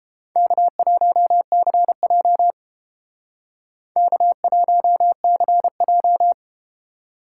Morse code, heard as a keyed tone, about 31 wpm, 700 Hz. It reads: K1CJ K1CJ